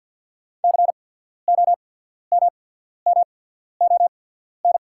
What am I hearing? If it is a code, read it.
Morse code, heard as a keyed tone, about 48 wpm, spaced as for 14 wpm, 690 Hz. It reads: COMMON
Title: 8x Character Spacing - Top 500 Words - 50wpm